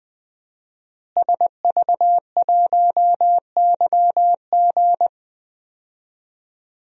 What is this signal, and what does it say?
Morse code, heard as a keyed tone, about 20 wpm, 700 Hz. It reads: SV1YG
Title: DXCC-100 Call Signs - 5 Characters - Repeated 2x - 20wpm